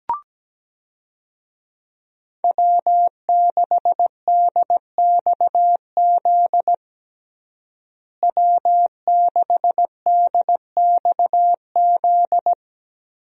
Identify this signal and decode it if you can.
Morse code, heard as a keyed tone, about 17 wpm, 700 Hz. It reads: W6DXZ W6DXZ
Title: USA Call Signs - 5 Characters - Repeated 2x - 17wpm